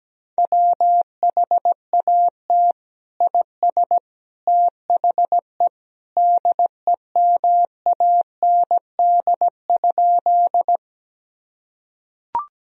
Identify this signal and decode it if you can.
Morse code, heard as a keyed tone, about 17 wpm, 700 Hz. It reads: WHAT IS THE DEMAND?
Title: Sentences from Top 1000 Words - Review Part 4 - 17wpm